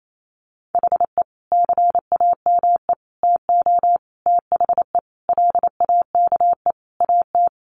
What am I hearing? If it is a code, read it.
Morse code, heard as a keyed tone, about 28 wpm, 700 Hz. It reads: HE CAME TO THE LAKE AT
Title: Sentences from Top 1100 Words 28wpm